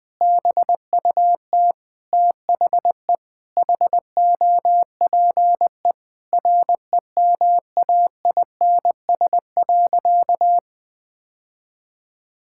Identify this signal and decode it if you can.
Morse code, heard as a keyed tone, about 20 wpm, 700 Hz. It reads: BUT THE HOPE REMAINS.
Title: Sentences with 2-Word Phrases - Spoken Beginning - 20wpm